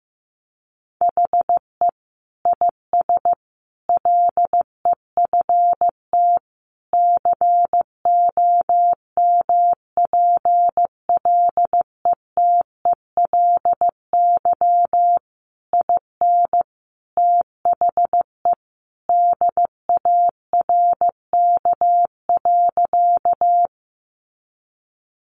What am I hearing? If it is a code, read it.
Morse code, heard as a keyed tone, about 15 wpm, 700 Hz. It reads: HE IS LEFT COMPLETELY IN THE DARK.
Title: Sentences from Top 1100 Words 15wpm